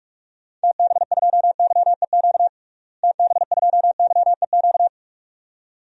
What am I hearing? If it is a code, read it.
Morse code, heard as a keyed tone, about 42 wpm, 700 Hz. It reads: TB2YEQ TB2YEQ
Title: DXCC-100 Call Signs - 6 Characters - Repeated 2x - 45wpm